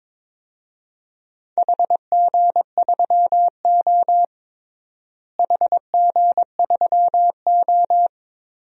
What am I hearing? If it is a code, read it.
Morse code, heard as a keyed tone, about 22 wpm, 700 Hz. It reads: HG3O HG3O
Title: DXCC-100 Call Signs - 4 Characters - Repeated 3x - 22wpm